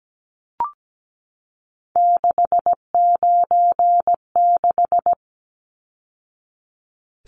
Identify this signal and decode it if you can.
Morse code, heard as a keyed tone, about 17 wpm, 700 Hz. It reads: 696
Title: Numbers - 3-Digit 17wpm